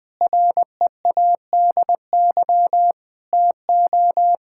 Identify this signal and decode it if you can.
Morse code, heard as a keyed tone, about 20 wpm, 700 Hz. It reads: READY TO
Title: Sets of 6 Words - 20wpm